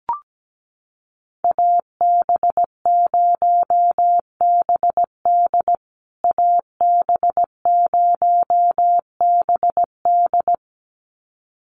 Morse code, heard as a keyed tone, about 17 wpm, 700 Hz. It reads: AB0BD AB0BD
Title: USA Call Signs Repeated 2x 17wpm